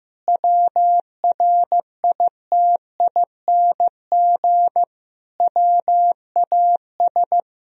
Morse code, heard as a keyed tone, about 15 wpm, 700 Hz. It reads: WRITING WAS
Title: Sentences from Top 1000 Words - Review Part 3 - 15wpm